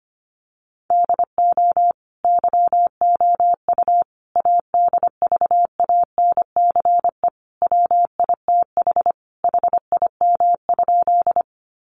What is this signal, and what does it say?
Morse code, heard as a keyed tone, about 25 wpm, 700 Hz. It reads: DO YOU ADVANCE WITH HIM?
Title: Sentences from Top 1400 Words 25wpm